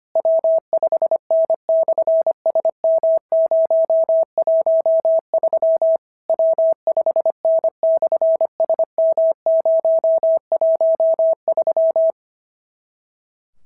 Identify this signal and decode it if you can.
Morse code, heard as a keyed tone, about 25 wpm, 650 Hz. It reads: W5N/SM013 W5N/SM013
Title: SOTA QSO Practice 25wpm